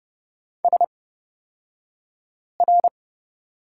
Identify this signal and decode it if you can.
Morse code, heard as a keyed tone, about 31 wpm, spaced as for 5 wpm, 700 Hz. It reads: SR
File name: Lesson 36d - BT - Two Characters 30wpm